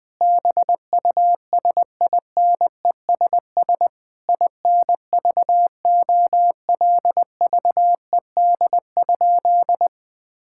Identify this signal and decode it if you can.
Morse code, heard as a keyed tone, about 20 wpm, 700 Hz. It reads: BUSINESS INVOLVED?
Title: Sentences from Top 1000 Words - Review Part 5 - 20wpm